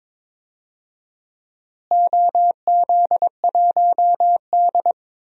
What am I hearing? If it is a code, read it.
Morse code, heard as a keyed tone, about 21 wpm, 700 Hz. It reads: OZ1D
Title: DXCC-100 Call Signs - 4 Characters - Repeated 2x - 22wpm